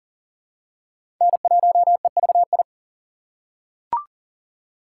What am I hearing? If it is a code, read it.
Morse code, heard as a keyed tone, about 40 wpm, 700 Hz. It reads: N1EVI
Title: USA Call Signs - 6x Character Spacing - 40wpm